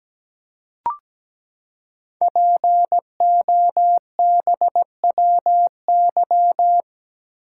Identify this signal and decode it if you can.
Morse code, heard as a keyed tone, about 17 wpm, 700 Hz. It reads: POBWY
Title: Code Groups - 5 Letters - 17wpm